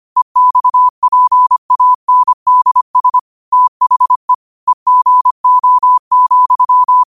Morse code, heard as a keyed tone, about 25 wpm, 1000 Hz. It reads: EXPANDS THE EGO,
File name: qotdfiles / 25wpm